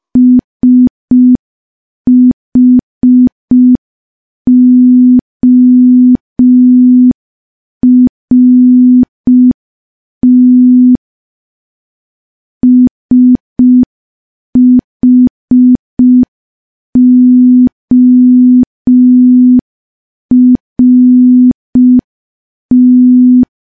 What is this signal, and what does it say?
Morse code, heard as a keyed tone, about 5 wpm, 250 Hz. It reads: SHORT SHORT
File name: morse / morse-training-05WPM